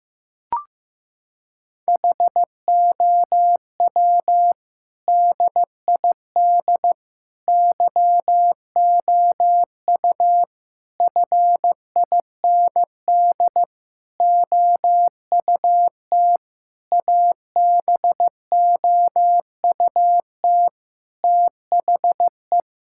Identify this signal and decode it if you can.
Morse code, heard as a keyed tone, about 15 wpm, 700 Hz. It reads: HOW DID YOU FIND OUT ABOUT THE
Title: Sentences from Top 1000 Words - Review Part 5 - 15wpm